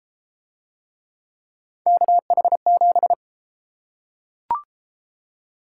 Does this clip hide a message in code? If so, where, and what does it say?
Yes: Morse code from 1.9 to 3.1 s, heard as a keyed tone, about 28 wpm, 700 Hz. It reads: KH7